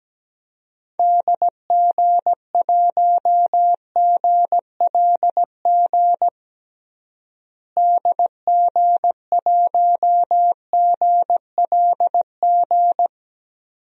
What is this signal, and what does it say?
Morse code, heard as a keyed tone, about 17 wpm, 700 Hz. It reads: DG1GLG DG1GLG
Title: DXCC-100 Call Signs - 6 Characters - Repeated 2x - 17wpm